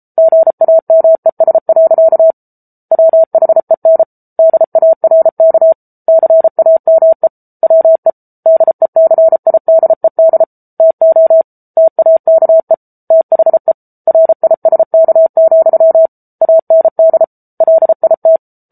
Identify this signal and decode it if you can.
Morse code, heard as a keyed tone, about 34 wpm, 650 Hz. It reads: GAMES. WHEN DARK CAME WE DECIDED TO TAKE THE RISK, AND LIT
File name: Book Two / 07-Chapter 7 - 33 WPM